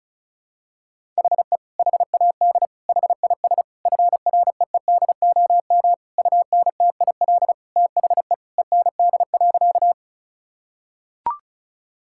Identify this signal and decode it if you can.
Morse code, heard as a keyed tone, about 34 wpm, 700 Hz. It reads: HE HAD HIS FREEDOM UNTIL THE END.
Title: Sentences from Top 1000 Words - Review Part 5 - 35wpm